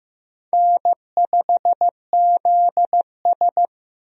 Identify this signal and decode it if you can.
Morse code, heard as a keyed tone, about 15 wpm, 700 Hz. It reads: N5ZS
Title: USA Call Signs - 4 Characters - Repeated 2x - 15wpm